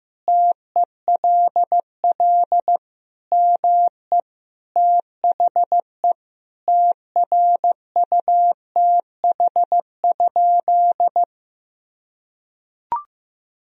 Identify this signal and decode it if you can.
Morse code, heard as a keyed tone, about 15 wpm, 700 Hz. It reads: TELL ME THE TRUTH?
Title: Sentences from Top 1000 Words - Review Part 3 - 15wpm